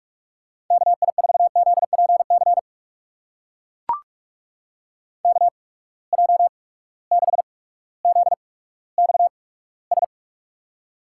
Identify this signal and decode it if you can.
Morse code, heard as a keyed tone, about 44 wpm, spaced as for 19 wpm, 700 Hz. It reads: KI4ZPC KJ6ZXS
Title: USA Call Signs - 6 Characters - 8x Character Spacing - 45wpm